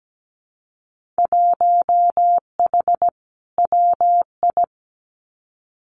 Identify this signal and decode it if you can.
Morse code, heard as a keyed tone, about 17 wpm, 700 Hz. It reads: 1H WI
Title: Winter Field Day Exchanges 17wpm